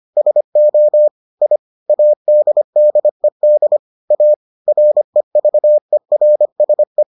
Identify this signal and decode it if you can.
Morse code, heard as a keyed tone, about 25 wpm, 600 Hz. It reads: SO I ADDED A REVERSE